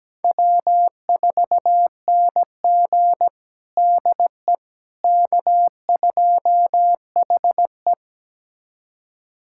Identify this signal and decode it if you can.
Morse code, heard as a keyed tone, about 17 wpm, 700 Hz. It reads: W4NG DE K2HE
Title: USA Callsign DE Callsign - 4 Characters - 17wpm